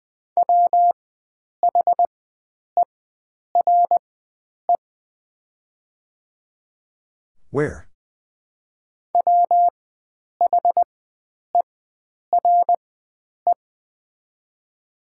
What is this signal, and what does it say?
Morse code, heard as a keyed tone, about 20 wpm, spaced as for 9 wpm, 700 Hz. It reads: WHERE WHERE
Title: 4x Character Spacing - Top 500 Words - 20wpm